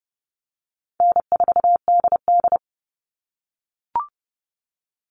Morse code, heard as a keyed tone, about 30 wpm, 700 Hz. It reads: N4DD
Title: All Cut Numbers - 4-Digit 30wpm